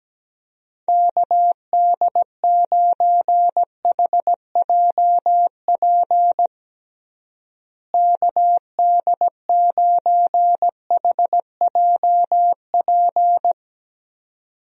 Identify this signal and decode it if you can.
Morse code, heard as a keyed tone, about 17 wpm, 700 Hz. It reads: KD9HJP KD9HJP
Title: USA Call Signs - 6 Characters - Repeated 2x - 17wpm